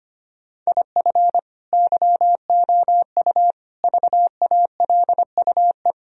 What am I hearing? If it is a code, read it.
Morse code, heard as a keyed tone, about 25 wpm, 700 Hz. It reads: IF YOU VALUE
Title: Sentences from Top 1000 Words - Review Part 3 - 25wpm